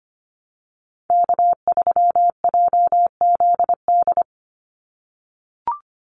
Morse code, heard as a keyed tone, about 25 wpm, 700 Hz. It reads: K3JZD
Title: NA SOTA Chaser Callsigns 25wpm